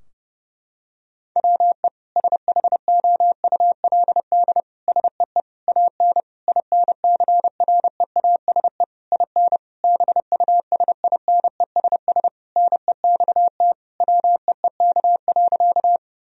Morse code, heard as a keyed tone, about 32 wpm, 700 Hz. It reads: WE SHOULD SEE AN INCREASE IN BUSINESS NEXT WEEK.
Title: Sentences from Top 1000 Words - Review Part 4 - 30wpm